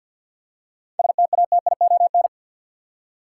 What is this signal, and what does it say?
Morse code, heard as a keyed tone, about 50 wpm, 700 Hz. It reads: STATION
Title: Seven Letter Words - 50wpm